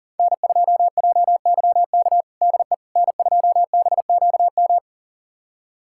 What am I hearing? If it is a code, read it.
Morse code, heard as a keyed tone, about 40 wpm, 700 Hz. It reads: N2JYK DE N2BQM
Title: USA Callsign DE Callsign - 5 Characters - 40wpm